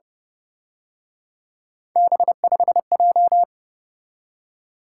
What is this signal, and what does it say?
Morse code, heard as a keyed tone, about 31 wpm, 700 Hz. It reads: B5J
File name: Speed-Racing - Three Letter-Number Combinations - 25wpm